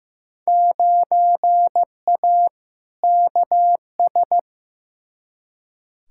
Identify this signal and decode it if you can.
Morse code, heard as a keyed tone, about 15 wpm, 700 Hz. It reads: 9A KS
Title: Field Day Exchanges Encore - 15wpm